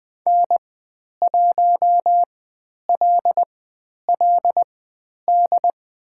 Morse code, heard as a keyed tone, about 20 wpm, spaced as for 10 wpm, 700 Hz. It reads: N1LLD
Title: 095 Lesson 16g - Letter c - Call Signs 20wpm